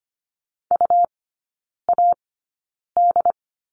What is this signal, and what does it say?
Morse code, heard as a keyed tone, about 25 wpm, spaced as for 9 wpm, 700 Hz. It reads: UAD
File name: Lesson 17e - Letter u - Three Characters 25wpm